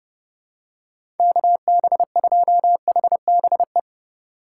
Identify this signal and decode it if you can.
Morse code, heard as a keyed tone, about 30 wpm, 700 Hz. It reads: KB2HBE
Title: USA Call Signs - 6 Characters - 2x Character Spacing - 30wpm